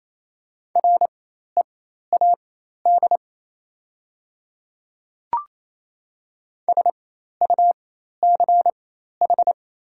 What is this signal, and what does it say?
Morse code, heard as a keyed tone, about 28 wpm, spaced as for 13 wpm, 700 Hz. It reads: READ SUCH
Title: 4x Character Spacing - Top 200 Words - 28wpm